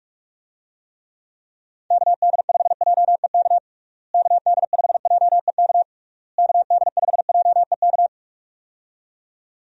Morse code, heard as a keyed tone, about 46 wpm, 700 Hz. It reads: KD5JEK KD5JEK KD5JEK
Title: USA Call Signs - 6 Characters - Repeated 3x - 45wpm